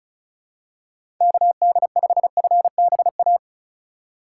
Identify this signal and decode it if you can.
Morse code, heard as a keyed tone, about 34 wpm, 700 Hz. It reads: KD5FBA
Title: K1USN SST QSO Practice - 35wpm